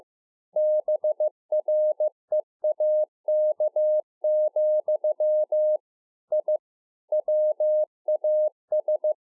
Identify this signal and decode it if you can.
Morse code, heard as a keyed tone, about 15 wpm, 610 Hz. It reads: BREAK, I WAS